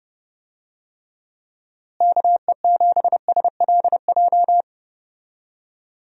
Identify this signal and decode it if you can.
Morse code, heard as a keyed tone, about 32 wpm, 700 Hz. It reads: KE7SLJ
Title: USA Call Signs - 6 Characters - Repeated 3x - 30wpm